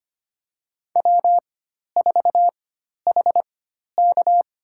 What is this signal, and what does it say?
Morse code, heard as a keyed tone, about 25 wpm, spaced as for 12 wpm, 700 Hz. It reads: W4HK
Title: USA Call Signs - 4 Characters - 4x Character Spacing - 25wpm